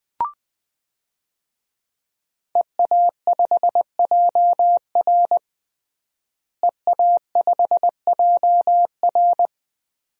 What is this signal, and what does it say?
Morse code, heard as a keyed tone, about 20 wpm, 700 Hz. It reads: EA5JR EA5JR
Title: DXCC-100 Call Signs - 5 Characters - Repeated 2x - 20wpm